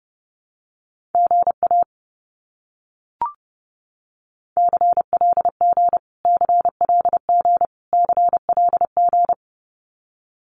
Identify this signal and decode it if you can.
Morse code, heard as a keyed tone, about 30 wpm, 700 Hz. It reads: GA CLG CLG CLG
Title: Abbreviations Repeated 3x - 30wpm